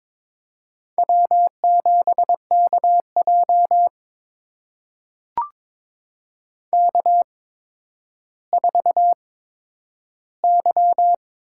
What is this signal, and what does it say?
Morse code, heard as a keyed tone, about 22 wpm, spaced as for 11 wpm, 700 Hz. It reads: W7KJ K4Y
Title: USA Call Signs - 4 Characters - 8x Character Spacing - 22wpm